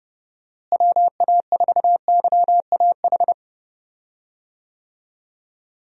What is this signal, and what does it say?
Morse code, heard as a keyed tone, about 30 wpm, 700 Hz. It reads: WA4YAH
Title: USA Call Signs - 6 Characters - Repeated 2x - 30wpm